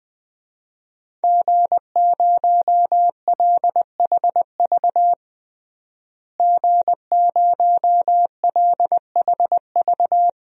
Morse code, heard as a keyed tone, about 20 wpm, 700 Hz. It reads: G0LHV G0LHV
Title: DXCC-100 Call Signs - 5 Characters - Repeated 3x - 20wpm